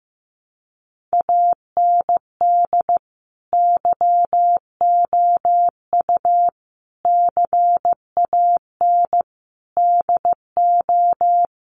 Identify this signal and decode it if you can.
Morse code, heard as a keyed tone, about 15 wpm, 700 Hz. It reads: AND YOU CAN DO
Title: Sentences from Top 500 Words 15wpm